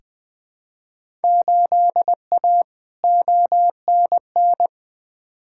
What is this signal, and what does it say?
Morse code, heard as a keyed tone, about 21 wpm, 700 Hz. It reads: 8A ONN
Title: Field Day Exchanges Encore - 20wpm